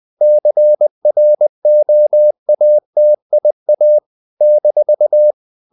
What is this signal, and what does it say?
Morse code, heard as a keyed tone, about 20 wpm, 600 Hz. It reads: CROATIA -
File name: CW / Practice / NationalCapitals20